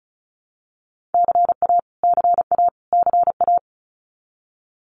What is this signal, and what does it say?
Morse code, heard as a keyed tone, about 34 wpm, 700 Hz. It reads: CA CA CA
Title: US States Repeated 3x 35wpm